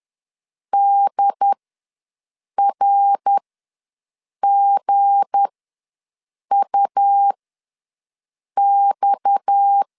Morse code, heard as a keyed tone, about 10 wpm, spaced as for 6 wpm, 800 Hz. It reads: DRGUX
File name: LEZIONI / 8 YJ45 / 8 YJ ws 10wpm 4043